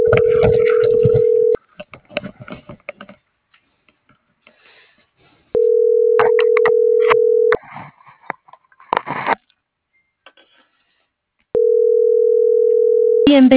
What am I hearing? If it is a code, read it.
Ringback tone (North America), 440 + 480 Hz.